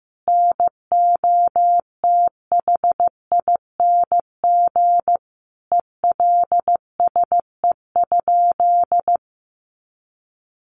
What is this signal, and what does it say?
Morse code, heard as a keyed tone, about 15 wpm, 700 Hz. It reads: NOTHING ELSE?
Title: Sentences from Top 800 Words 15wpm